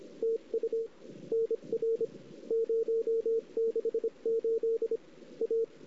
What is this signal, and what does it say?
Morse code, heard as a keyed tone, about 29 wpm, spaced as for 25 wpm, 410 Hz. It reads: TU NR 068 A
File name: CW / 2008-ss-cw-k5tr-op-n5rz -03:00